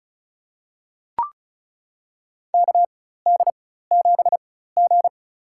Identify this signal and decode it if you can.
Morse code, heard as a keyed tone, about 36 wpm, spaced as for 16 wpm, 700 Hz. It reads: KD7G
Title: USA Call Signs - 4 Characters - 4x Character Spacing - 35wpm